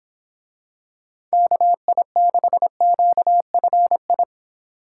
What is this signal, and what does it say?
Morse code, heard as a keyed tone, about 24 wpm, 700 Hz. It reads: KI6QFI